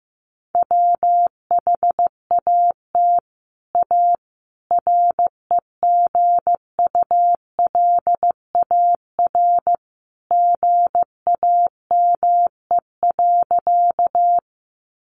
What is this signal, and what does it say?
Morse code, heard as a keyed tone, about 15 wpm, 700 Hz. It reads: WHAT A REGULAR GAME.
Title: Sentences from Top 900 Words 15wpm